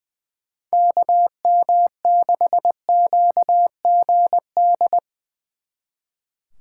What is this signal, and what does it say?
Morse code, heard as a keyed tone, about 23 wpm, 700 Hz. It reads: KM6QGD
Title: Speed Racing - USA 2x3 Call Signs 20wpm